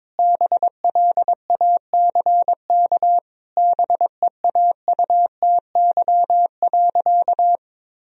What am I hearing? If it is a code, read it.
Morse code, heard as a keyed tone, about 22 wpm, 700 Hz. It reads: BLACK BEAUTY.
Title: Sentences from Top 1000 Words - Review Part 2 - 22wpm